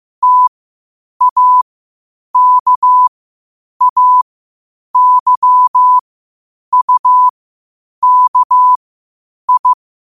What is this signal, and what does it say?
Morse code, heard as a keyed tone, about 15 wpm, spaced as for 8 wpm, 1000 Hz. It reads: TAKAYUKI
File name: qotdfiles / 8wpm